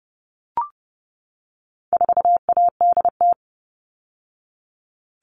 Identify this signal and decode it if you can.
Morse code, heard as a keyed tone, about 30 wpm, 700 Hz. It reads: 4ADT
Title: All Cut Numbers - 4-Digit 30wpm